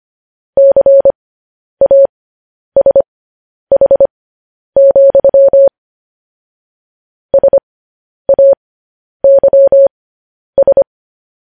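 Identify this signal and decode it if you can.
Morse code, heard as a keyed tone, about 25 wpm, spaced as for 10 wpm, 550 Hz. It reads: CASH, SAYS
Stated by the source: Morse code news headlines at 10 wpm.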